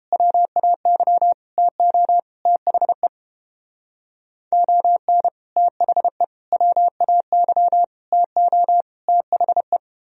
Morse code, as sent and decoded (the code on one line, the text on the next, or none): .-- .- -.-- / - --- / - .... / --- -. / - .... / .-- .- -.-- / - --- / - .... .
WAY TO THE ON THE WAY TO THE